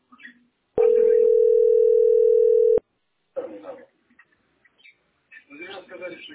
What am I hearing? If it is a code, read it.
Ringback tone (North America), 440 + 480 Hz, with speech.
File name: 3f14cd75-226b-4dda-849e